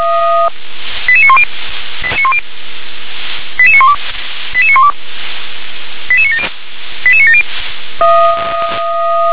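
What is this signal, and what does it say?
Five-tone selective call (ZVEI), heard as quick runs of five tones, tones of about 70 ms. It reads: 88910 88912 88912 88970 88970
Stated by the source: ZVEI five-tone selective call received on an SDR, channel 357.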